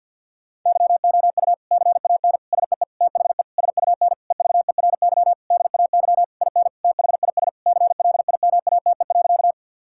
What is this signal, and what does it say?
Morse code, heard as a keyed tone, about 52 wpm, 700 Hz. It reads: YOU CAN SEE THE SUN EVERY DAY IN THIS CLIMATE.
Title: Sentences from Top 1600 Words 50wpm